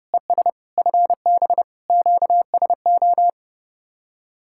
Morse code, heard as a keyed tone, about 29 wpm, 700 Hz. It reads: ES FB QSO